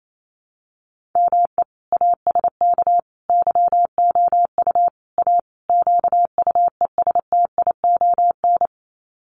Morse code, heard as a keyed tone, about 28 wpm, 700 Hz. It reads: ME ASK YOU A QUESTION
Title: Sets of 5 Words Encore2 28wpm